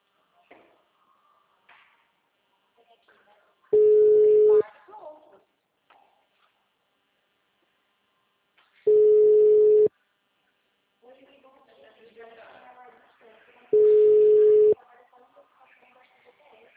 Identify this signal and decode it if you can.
Ringback tone (Europe, 425 Hz), with speech.
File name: out- -2002- - - .166